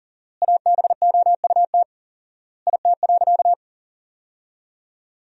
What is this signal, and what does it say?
Morse code, heard as a keyed tone, about 37 wpm, 700 Hz. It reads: ABOUT IT.
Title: 4x Word-Spacing - Sentences from Top 200 Words - 40wpm